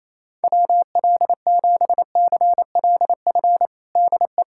Morse code, heard as a keyed tone, about 28 wpm, 700 Hz. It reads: WL7CLF DE
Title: USA Callsign DE Callsign - 6 Characters - 28wpm